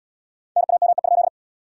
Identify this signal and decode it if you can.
Morse code, heard as a keyed tone, about 77 wpm, 700 Hz. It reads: NAME?